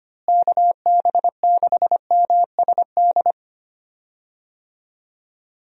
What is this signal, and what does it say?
Morse code, heard as a keyed tone, about 25 wpm, 700 Hz. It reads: KB6MSD
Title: USA Call Signs - 6 Characters - Repeated 3x - 25wpm